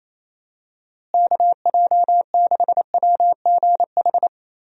Morse code, heard as a keyed tone, about 28 wpm, 700 Hz. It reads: KJ6WGH
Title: USA Call Signs - 6 Characters - Repeated 2x - 28wpm